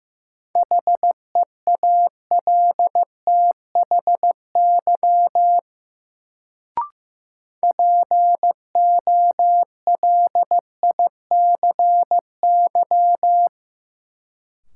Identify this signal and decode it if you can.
Morse code, heard as a keyed tone, about 15 wpm, 700 Hz. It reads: HEALTHY POLICY
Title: Sentences from Top 1000 Words - Review Part 5 - New Word Drill - 15wpm